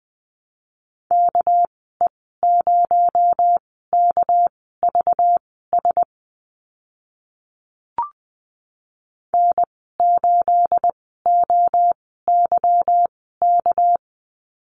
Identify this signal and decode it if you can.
Morse code, heard as a keyed tone, about 20 wpm, spaced as for 14 wpm, 700 Hz. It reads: KE0KVS N8OYK
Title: 219 Lesson 35g - Number 0 - Call Signs 20wpm